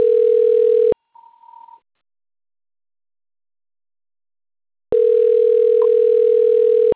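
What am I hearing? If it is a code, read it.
Ringback tone (North America), 440 + 480 Hz.
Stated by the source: Outbound telephone call recorded on the Americana 1 system.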